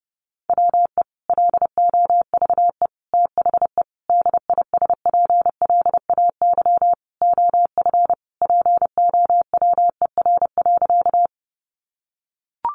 Morse code, heard as a keyed tone, about 30 wpm, 700 Hz. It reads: WE LOVE THE DISPLAY OF POWER.
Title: Sentences from Top 1200 Words 30wpm